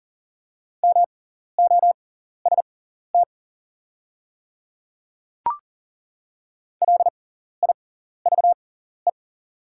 Morse code, heard as a keyed tone, about 40 wpm, spaced as for 14 wpm, 690 Hz. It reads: MOST LIVE
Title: 6x Character Spacing - Top 200 Words - 40wpm